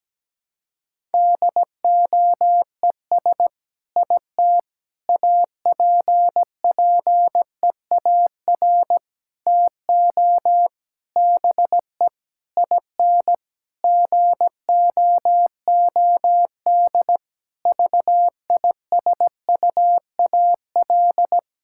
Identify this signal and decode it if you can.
Morse code, heard as a keyed tone, about 17 wpm, 700 Hz. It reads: DOES IT APPEAR TO BE IN GOOD VISUAL
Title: Sentences from Top 1000 Words - Review Part 6 - 17wpm